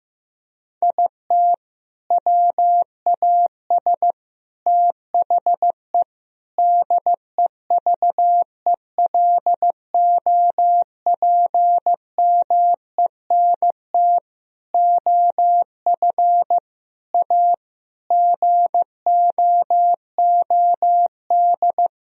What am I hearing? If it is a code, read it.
Morse code, heard as a keyed tone, about 15 wpm, 700 Hz. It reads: IT WAS THE DEVELOPMENT OF A GOOD
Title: Sentences from Top 1000 Words - Review Part 4 - 15wpm